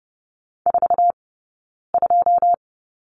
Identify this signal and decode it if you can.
Morse code, heard as a keyed tone, about 30 wpm, spaced as for 9 wpm, 690 Hz. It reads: 42
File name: Lesson 17d - Letter u - Two Characters 30wpm